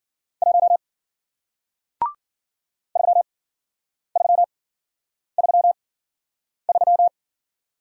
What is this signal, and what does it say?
Morse code, heard as a keyed tone, about 54 wpm, spaced as for 9 wpm, 700 Hz. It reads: 1 3333